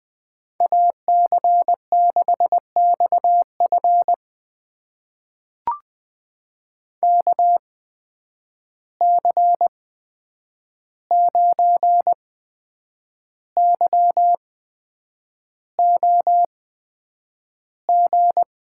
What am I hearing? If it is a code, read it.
Morse code, heard as a keyed tone, about 20 wpm, spaced as for 8 wpm, 700 Hz. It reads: AC6XF KC9YOG
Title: USA Call Signs - 8x Character Spacing - 20wpm